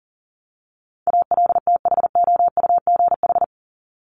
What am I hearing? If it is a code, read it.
Morse code, heard as a keyed tone, about 40 wpm, 700 Hz. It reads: ALTHOUGH
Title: Top 501-750 Words 40wpm